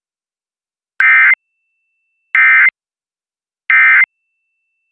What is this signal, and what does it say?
Emergency Alert System (SAME) tones, heard as harsh buzzing bursts. It is heard from 1.0 to 4.0 s.